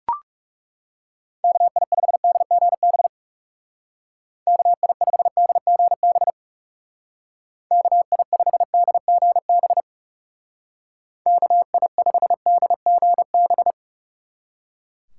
Morse code, heard as a keyed tone, about 37 wpm, 700 Hz. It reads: KI5DGB KI5DGB KI5DGB KI5DGB
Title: Speed Racing - USA 2x3 Call Signs 30wpm